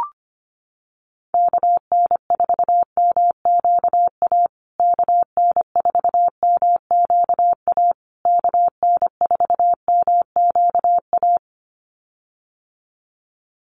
Morse code, heard as a keyed tone, about 25 wpm, 700 Hz. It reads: KN4MQA KN4MQA KN4MQA
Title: USA Call Signs Repeated 3x 25wpm